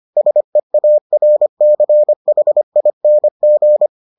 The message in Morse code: SEARCHING

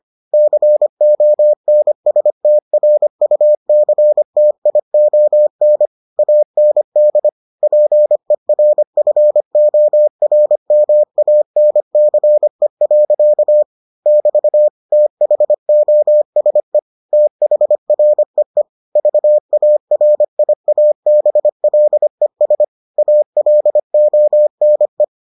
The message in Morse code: CONSTRUCTION AND PERFORMANCE. <BT> THOSE THREE VARIABLES ALONE

